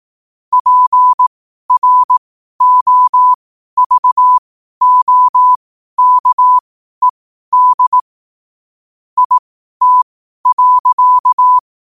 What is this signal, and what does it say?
Morse code, heard as a keyed tone, about 18 wpm, spaced as for 12 wpm, 1000 Hz. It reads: PROVOKED IT.
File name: qotdfiles / 12wpm